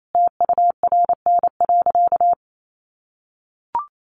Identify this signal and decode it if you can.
Morse code, heard as a keyed tone, about 28 wpm, 700 Hz. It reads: TURN.
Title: Sentences from Top 600 Words 28wpm